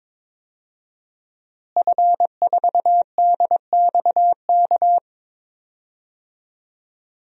Morse code, heard as a keyed tone, about 22 wpm, 700 Hz. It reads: F4DXK
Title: DXCC-100 Call Signs - 5 Characters - Repeated 2x - 22wpm